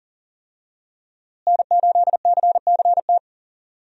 Morse code, heard as a keyed tone, about 40 wpm, 700 Hz. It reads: N8CCT
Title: USA Call Signs - 5 Characters - Repeated 3x - 40wpm